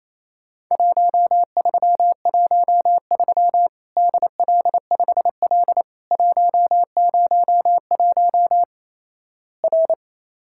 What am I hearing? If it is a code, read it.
Morse code, heard as a keyed tone, about 28 wpm, 700 Hz. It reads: 1313 DL5L 101 R
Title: Worked All Europe DX Contest - QTC Exchange - 28wpm